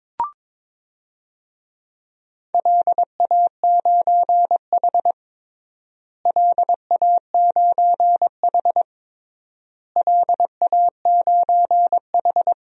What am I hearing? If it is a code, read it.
Morse code, heard as a keyed tone, about 22 wpm, 700 Hz. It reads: LA9H LA9H LA9H
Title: DXCC-100 Call Signs - 4 Characters - Repeated 3x - 22wpm